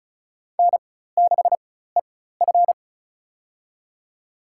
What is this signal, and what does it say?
Morse code, heard as a keyed tone, about 36 wpm, spaced as for 16 wpm, 700 Hz. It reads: N6EF
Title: USA Call Signs - 4 Characters - 4x Character Spacing - 35wpm